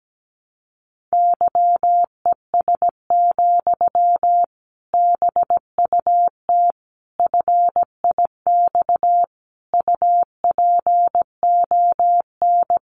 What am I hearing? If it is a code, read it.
Morse code, heard as a keyed tone, about 17 wpm, 700 Hz. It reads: YES, BUT FIX UPON
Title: Sentences from Top 1400 Words 17wpm